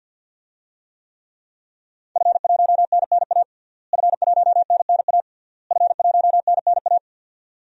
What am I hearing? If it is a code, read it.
Morse code, heard as a keyed tone, about 52 wpm, 700 Hz. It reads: F1NNA F1NNA F1NNA
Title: DXCC-100 Call Signs - 5 Characters - Repeated 3x - 50wpm